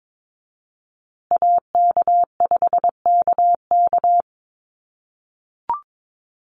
Morse code, heard as a keyed tone, about 22 wpm, 700 Hz. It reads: AK5KK
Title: USA Call Signs Repeated 2x 22wpm